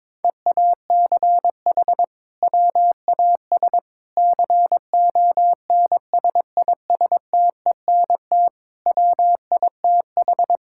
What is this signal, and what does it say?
Morse code, heard as a keyed tone, about 22 wpm, 700 Hz. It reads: EACH WAS CONSISTENT WITH
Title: Sentences from Top 1600 Words 22wpm